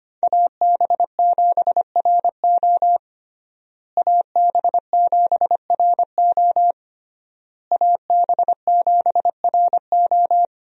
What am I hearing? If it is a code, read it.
Morse code, heard as a keyed tone, about 25 wpm, 700 Hz. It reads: AB7RO AB7RO AB7RO
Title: USA Call Signs - 5 Characters - Repeated 3x - 25wpm